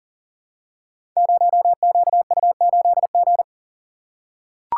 Morse code, heard as a keyed tone, about 39 wpm, 690 Hz. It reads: 0QU8G
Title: Code Groups - 5 Letter:Numbers - 40wpm